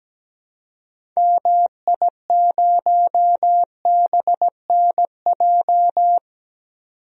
Morse code, heard as a keyed tone, about 17 wpm, 700 Hz. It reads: MI0BNJ
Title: DXCC-100 Call Signs - 6 Characters - 17wpm